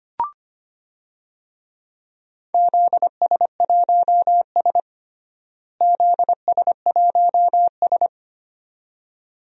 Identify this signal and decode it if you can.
Morse code, heard as a keyed tone, about 25 wpm, 700 Hz. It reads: ZS1S ZS1S
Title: DXCC-100 Call Signs - 4 Characters - Repeated 2x - 25wpm